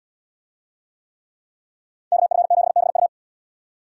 Morse code, heard as a keyed tone, about 78 wpm, 700 Hz. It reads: KF7BU